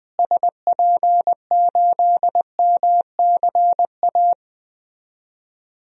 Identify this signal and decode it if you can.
Morse code, heard as a keyed tone, about 20 wpm, 700 Hz. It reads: SP8MCA
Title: DXCC-100 Call Signs - 6 Characters - Repeated 3x - 20wpm